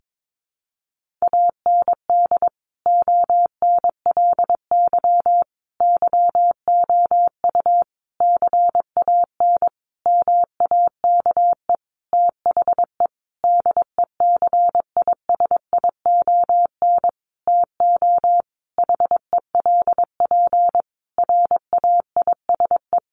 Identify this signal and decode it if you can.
Morse code, heard as a keyed tone, about 22 wpm, 700 Hz. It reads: AND ONLY YOU CAN MAKE THE DECISION TO HELP RAISE
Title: Sentences from Top 1400 Words 22wpm